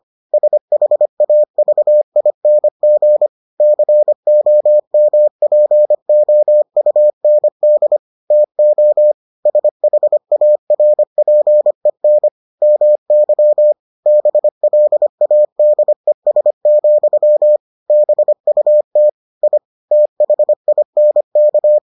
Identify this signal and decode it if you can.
Morse code, heard as a keyed tone, about 25 wpm, 600 Hz. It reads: SHAVING COMPOUND TO SHARPEN MY BLADES, BUT I THINK